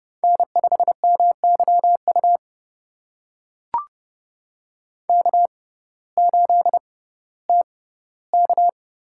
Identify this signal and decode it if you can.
Morse code, heard as a keyed tone, about 30 wpm, spaced as for 17 wpm, 700 Hz. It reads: N5MYU K8TK
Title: USA Call Signs - 5 Characters - 6x Character Spacing - 30wpm